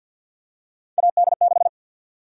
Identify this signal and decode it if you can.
Morse code, heard as a keyed tone, about 53 wpm, 700 Hz. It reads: AD6
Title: USA Call Sign Prefixes - 50wpm